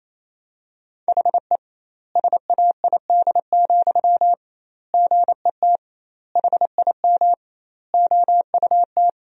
Morse code, heard as a keyed tone, about 28 wpm, 700 Hz. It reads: HE SAID, GET HIM OUT
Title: 2x Word-Spacing - Sentences from Top 400 Words - 28wpm